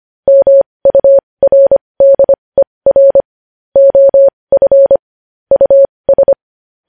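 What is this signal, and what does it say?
Morse code, heard as a keyed tone, about 25 wpm, spaced as for 20 wpm, 550 Hz. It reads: MURDER OF US
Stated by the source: Morse code news headlines practice at 20 wpm.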